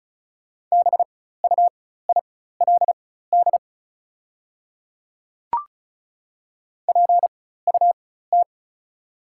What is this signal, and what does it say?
Morse code, heard as a keyed tone, about 34 wpm, spaced as for 16 wpm, 700 Hz. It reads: BUILD PUT
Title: 4x Character Spacing - Top 200 Words - 35wpm